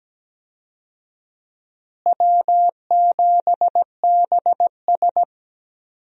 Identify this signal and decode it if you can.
Morse code, heard as a keyed tone, about 17 wpm, 700 Hz. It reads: W7BS
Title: USA Call Signs - 4 Characters - Repeated 2x - 17wpm